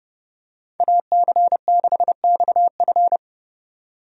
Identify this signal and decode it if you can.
Morse code, heard as a keyed tone, about 30 wpm, 700 Hz. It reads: AC6XF
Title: USA Call Signs - 5 Characters - 30wpm